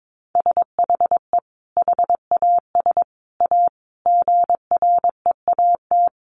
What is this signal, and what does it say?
Morse code, heard as a keyed tone, about 22 wpm, 700 Hz. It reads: SHE HAS A GREAT
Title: Sentences from Top 1100 Words 22wpm